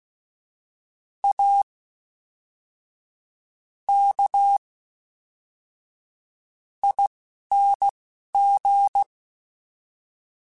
Morse code, heard as a keyed tone, about 16 wpm, spaced as for 5 wpm, 800 Hz. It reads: AKING
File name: Special INGs / Special INGs-mp3s / Fast ing @10